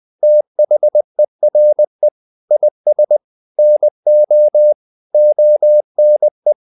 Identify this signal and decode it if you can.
Morse code, heard as a keyed tone, about 20 wpm, 600 Hz. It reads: THERE IS NO ONE
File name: the old man and the sea part 2 20wpm